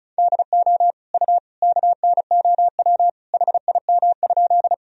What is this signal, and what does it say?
Morse code, heard as a keyed tone, about 36 wpm, 700 Hz. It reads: DO U KNOW HIM?
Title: Sentences from Top 100 Words and Abbreviations 35wpm